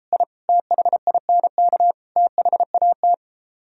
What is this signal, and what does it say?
Morse code, heard as a keyed tone, about 34 wpm, 700 Hz. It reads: I THINK THAT